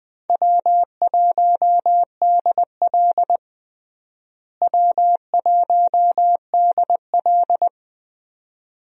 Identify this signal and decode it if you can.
Morse code, heard as a keyed tone, about 21 wpm, 700 Hz. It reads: W1DL W1DL
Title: USA Call Signs - 4 Characters - Repeated 2x - 20wpm